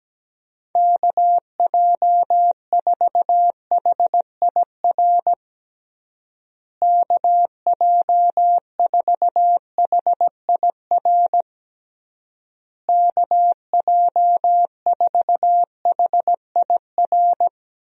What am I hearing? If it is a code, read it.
Morse code, heard as a keyed tone, about 17 wpm, 700 Hz. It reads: KJ4HIR KJ4HIR KJ4HIR
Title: USA Call Signs - 6 Characters - Repeated 3x - 17wpm